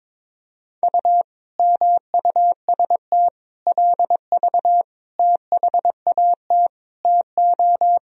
Morse code, heard as a keyed tone, about 21 wpm, 700 Hz. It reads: U MUST LV THAT TO
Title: Sentences from Top 400 Words and Abbreviations 22wpm